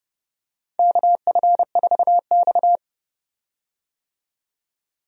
Morse code, heard as a keyed tone, about 30 wpm, 700 Hz. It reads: KF4X
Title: USA Call Signs - 4 Characters - 30wpm